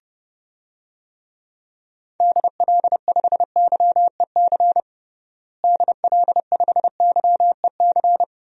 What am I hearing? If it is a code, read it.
Morse code, heard as a keyed tone, about 31 wpm, 700 Hz. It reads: DL5YEC DL5YEC
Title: DXCC-100 Call Signs - 6 Characters - Repeated 2x - 30wpm